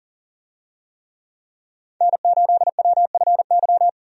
Morse code, heard as a keyed tone, about 40 wpm, 700 Hz. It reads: N8WFY
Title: USA Call Signs - 5 Characters - 40wpm